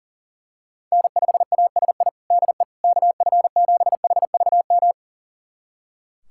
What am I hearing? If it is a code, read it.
Morse code, heard as a keyed tone, about 40 wpm, 700 Hz. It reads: N5ASI DE KF7HVM
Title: USA Callsign DE Callsign 40wpm